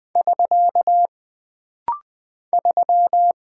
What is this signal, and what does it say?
Morse code, heard as a keyed tone, about 20 wpm, 700 Hz. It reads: <SK> 3
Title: Single Character (Letter:Number:Punctuation:Prosign) - Rapid-Fire - 20wpm